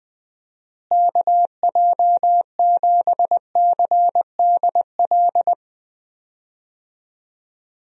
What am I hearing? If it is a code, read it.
Morse code, heard as a keyed tone, about 20 wpm, 700 Hz. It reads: KJ7CDL
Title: USA Call Signs - 6 Characters - Repeated 3x - 20wpm